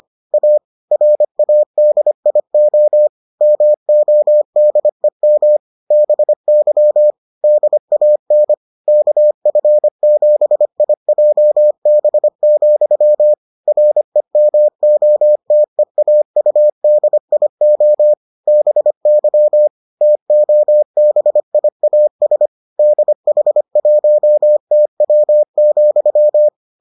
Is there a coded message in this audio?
Yes: Morse code from 0.3 to 26.5 s, heard as a keyed tone, about 25 wpm, 600 Hz. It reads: A RADIO MODEM BY DAN KF7IJB, REMOTEAUDIO BY TOBIAS DH1TW,